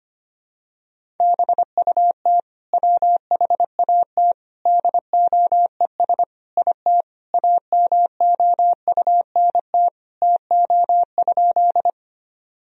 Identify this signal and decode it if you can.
Morse code, heard as a keyed tone, about 25 wpm, 700 Hz. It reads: BUT WHAT DOES IT AMOUNT TO?
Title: Sentences from Top 1000 Words - Review Part 4 - 25wpm